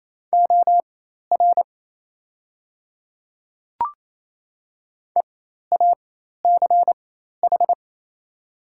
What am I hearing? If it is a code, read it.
Morse code, heard as a keyed tone, about 28 wpm, spaced as for 13 wpm, 700 Hz. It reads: OR EACH
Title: 4x Character Spacing - Top 100 Words - 28wpm